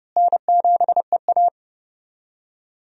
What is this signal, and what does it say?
Morse code, heard as a keyed tone, about 30 wpm, 700 Hz. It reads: N7EA
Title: USA Call Signs - 4 Characters - Repeated 2x - 30wpm